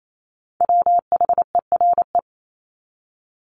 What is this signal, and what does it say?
Morse code, heard as a keyed tone, about 28 wpm, 700 Hz. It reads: WHERE
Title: Top 300 Words 28wpm